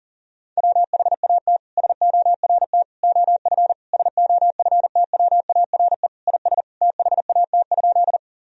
Morse code, heard as a keyed tone, about 37 wpm, 700 Hz. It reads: WHAT SORT OF SOFTWARE IS THAT?
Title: Sentences from Top 1000 Words - Review Part 3 - 40wpm